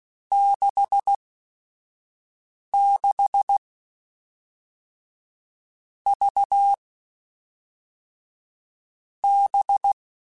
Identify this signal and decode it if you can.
Morse code, heard as a keyed tone, about 16 wpm, spaced as for 5 wpm, 800 Hz. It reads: 66 V B
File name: Groups@5 / Groups@5 mp3s / VB46@5